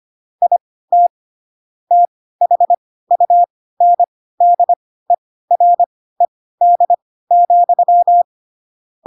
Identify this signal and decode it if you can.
Morse code, heard as a keyed tone, about 25 wpm, spaced as for 16 wpm, 700 Hz. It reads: IT THUNDERED,